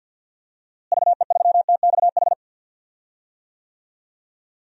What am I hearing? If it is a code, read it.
Morse code, heard as a keyed tone, about 51 wpm, 700 Hz. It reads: VE3TXH
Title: DXCC-100 Call Signs - 6 Characters - Repeated 3x - 50wpm